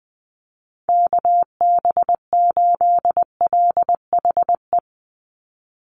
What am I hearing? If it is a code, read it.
Morse code, heard as a keyed tone, about 20 wpm, 700 Hz. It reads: KB8LHE
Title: USA Call Signs 20wpm